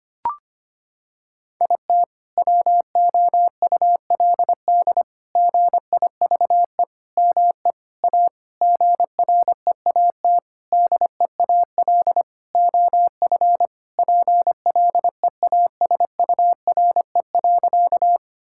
Morse code, heard as a keyed tone, about 25 wpm, 700 Hz. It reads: IT WOULD GIVE ME A GREAT DEAL OF PLEASURE.
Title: Sentences from Top 1600 Words 25wpm